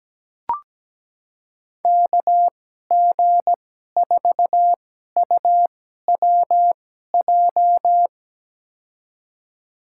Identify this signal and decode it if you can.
Morse code, heard as a keyed tone, about 17 wpm, spaced as for 12 wpm, 700 Hz. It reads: KG4UWJ
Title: USA Call Signs - 6 Characters - 2x Character Spacing - 17wpm